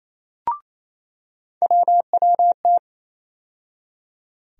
Morse code, heard as a keyed tone, about 28 wpm, 700 Hz. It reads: WWT
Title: USA Call Sign Postfixes - 3 Letters 28wpm